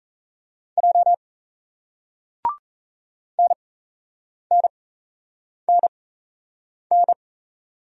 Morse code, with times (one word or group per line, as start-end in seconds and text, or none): 0.8-1.1 J
3.4-7.1 NNNN